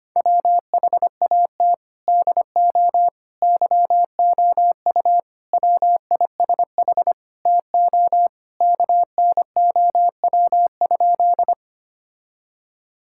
Morse code, heard as a keyed tone, about 25 wpm, 700 Hz. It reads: WHAT DO YOU WISH TO KNOW?
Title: Sentences from Top 1600 Words 25wpm